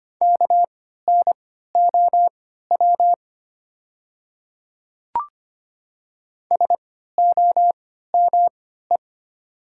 Morse code, heard as a keyed tone, about 25 wpm, spaced as for 14 wpm, 700 Hz. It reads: KNOW SOME
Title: 3x Character Spacing - Top 100 Words - 25wpm